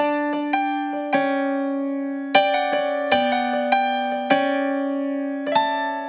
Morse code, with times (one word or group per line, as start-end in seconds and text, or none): none